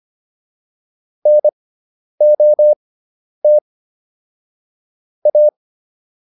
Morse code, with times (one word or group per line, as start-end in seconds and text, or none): 1.3-3.6 NOT
5.3-5.5 A